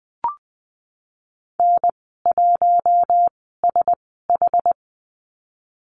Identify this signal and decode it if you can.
Morse code, heard as a keyed tone, about 19 wpm, spaced as for 14 wpm, 700 Hz. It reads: N1SH
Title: 193 Lesson 31g - Letter v - Call Signs 20wpm